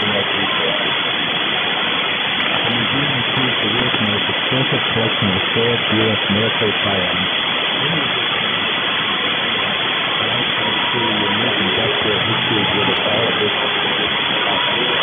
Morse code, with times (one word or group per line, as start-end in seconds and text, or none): none